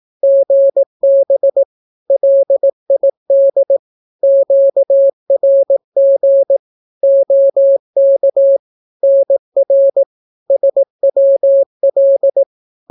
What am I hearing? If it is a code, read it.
Morse code, heard as a keyed tone, about 18 wpm, 550 Hz. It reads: GB LID QRG OK NR SWL
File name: naukapodstaw / skroty / skroty